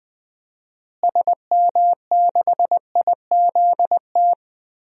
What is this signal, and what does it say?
Morse code, heard as a keyed tone, about 20 wpm, 700 Hz. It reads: SM6IZT
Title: DXCC-100 Call Signs - 6 Characters - Repeated 3x - 20wpm